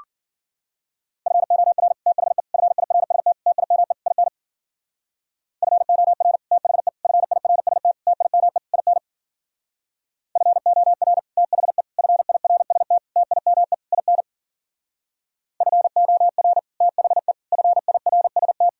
Morse code, heard as a keyed tone, about 51 wpm, 700 Hz. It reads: FOR THE FIRST TIME IN FOR THE FIRST TIME IN FOR THE FIRST TIME IN FOR THE FIRST